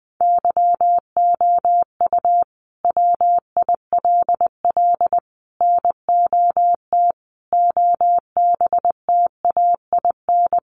Morse code, heard as a keyed tone, about 19 wpm, 700 Hz. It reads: YOU WILL NOT OBTAIN
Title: Sentences from Top 1300 Words 20wpm